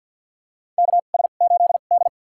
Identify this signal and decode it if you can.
Morse code, heard as a keyed tone, about 52 wpm, spaced as for 36 wpm, 700 Hz. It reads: KS8D
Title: USA Call Signs - 4 Characters - 2x Character Spacing - 50wpm